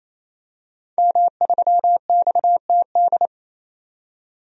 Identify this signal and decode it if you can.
Morse code, heard as a keyed tone, about 28 wpm, 700 Hz. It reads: M3XTD
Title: DXCC-100 Call Signs - 5 Characters - 28wpm